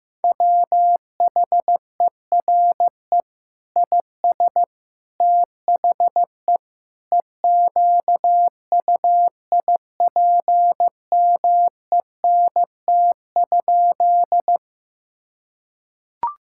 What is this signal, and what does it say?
Morse code, heard as a keyed tone, about 16 wpm, 700 Hz. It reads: WHERE IS THE EQUIPMENT?
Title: Sentences from Top 1000 Words - Review Part 5 - 15wpm